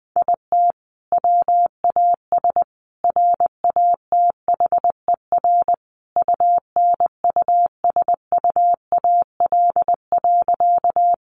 Morse code, heard as a keyed tone, about 20 wpm, 700 Hz. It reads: IT WAS RATHER UNUSUAL.
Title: Sentences from Top 1400 Words 20wpm